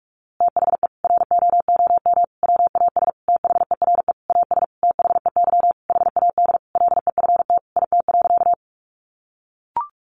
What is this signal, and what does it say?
Morse code, heard as a keyed tone, about 44 wpm, 700 Hz. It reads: THE ROOM WAS THERE AS THEY HAD LEFT IT.
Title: A Sound Of Thunder 45wpm